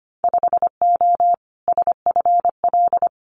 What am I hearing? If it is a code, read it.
Morse code, heard as a keyed tone, about 25 wpm, 700 Hz. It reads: 5O SFL
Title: Winter Field Day Exchanges 25wpm